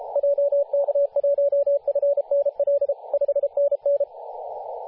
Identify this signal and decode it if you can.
Morse code, heard as a keyed tone, about 33 wpm, 570 Hz, with radio noise. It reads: JK1FNL 5NN
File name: posts / 2017-jan-27-0632z-a35w-14-031mhz-cw